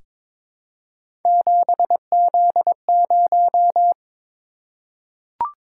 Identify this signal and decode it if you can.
Morse code, heard as a keyed tone, about 22 wpm, 700 Hz. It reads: 7Z0
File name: practice / 22wpm / Three Letter-Number Combinations - 22wpm